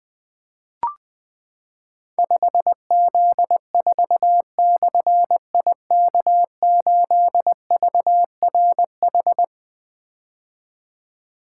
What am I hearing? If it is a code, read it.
Morse code, heard as a keyed tone, about 20 wpm, 700 Hz. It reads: 5Z4/IK8VRH
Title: Difficult Call Signs 20wpm